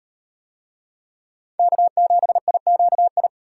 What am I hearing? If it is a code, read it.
Morse code, heard as a keyed tone, about 35 wpm, 700 Hz. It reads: K7IQI